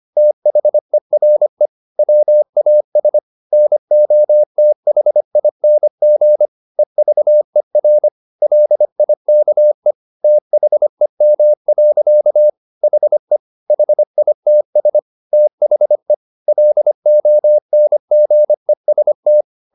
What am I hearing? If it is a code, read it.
Morse code, heard as a keyed tone, about 25 wpm, 600 Hz. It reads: THERE WAS NOTHING EVER LIKE THEM. HE HITS THE LONGEST
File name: the old man and the sea part 1 25wpm